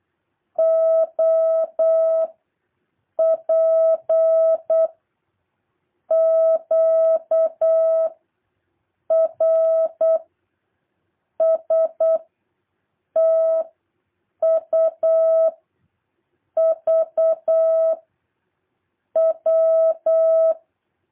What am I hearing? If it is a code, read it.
Morse code, heard as a keyed tone, about 8 wpm, 650 Hz. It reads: O P Q R S T U V W
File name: jh8cbh / 2023cw